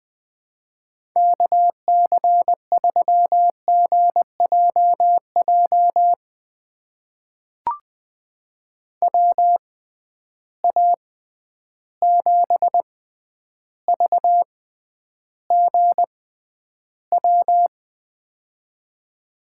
Morse code, heard as a keyed tone, about 21 wpm, spaced as for 10 wpm, 700 Hz. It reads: KC3GJJ WA7VGW
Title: USA Call Signs - 6 Characters - 6x Character Spacing - 20wpm